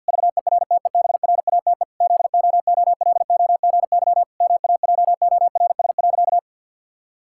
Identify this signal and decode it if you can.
Morse code, heard as a keyed tone, about 50 wpm, 700 Hz. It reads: VERTEBRATE ZOOLOGY MAJORS.
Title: Pangram Challenge v2 - 50wpm